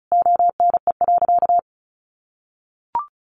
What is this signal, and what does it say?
Morse code, heard as a keyed tone, about 34 wpm, 700 Hz. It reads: ONE.
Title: Sentences from Top 1400 Words 35wpm